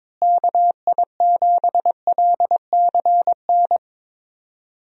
Morse code, heard as a keyed tone, about 22 wpm, 700 Hz. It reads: KI7LCN
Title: USA Call Signs - 6 Characters - Repeated 2x - 22wpm